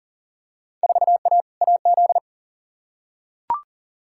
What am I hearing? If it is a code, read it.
Morse code, heard as a keyed tone, about 40 wpm, 700 Hz. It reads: 4A AZ
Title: Field Day Exchanges - 40wpm